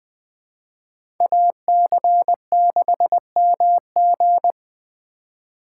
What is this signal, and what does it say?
Morse code, heard as a keyed tone, about 20 wpm, 700 Hz. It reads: AC6MG
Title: USA Call Signs - 4x Character Spacing - 20wpm